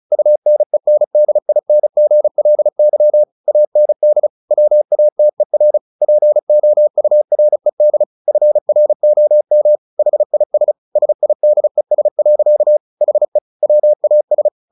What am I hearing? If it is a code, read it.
Morse code, heard as a keyed tone, about 34 wpm, 600 Hz. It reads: UNENDINGLY AND WATER POURED FROM HIS SIDES. HE WAS
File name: the old man and the sea / the old man and the sea 1 35wpm